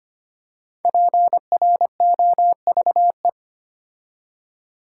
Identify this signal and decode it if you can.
Morse code, heard as a keyed tone, about 26 wpm, 700 Hz. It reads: PROVE
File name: 5 Letter Words - First 3 Letters Given - 25wpm